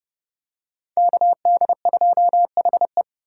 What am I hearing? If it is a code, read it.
Morse code, heard as a keyed tone, about 30 wpm, 700 Hz. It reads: KD2HE
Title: K1USN SST QSO Practice - 30wpm